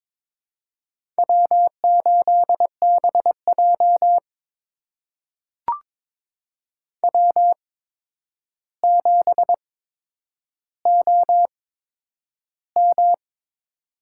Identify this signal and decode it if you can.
Morse code, heard as a keyed tone, about 22 wpm, spaced as for 9 wpm, 700 Hz. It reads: W8BJ W7OM
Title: USA Call Signs - 4 Characters - 8x Character Spacing - 22wpm